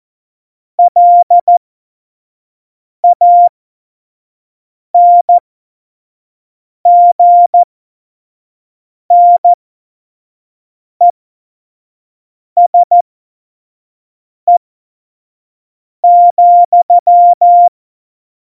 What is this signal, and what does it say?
Morse code, heard as a keyed tone, about 14 wpm, spaced as for 5 wpm, 700 Hz. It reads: LANGNESE,
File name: Morsekurs / Witze / Witz 23. April 2019, 25 F70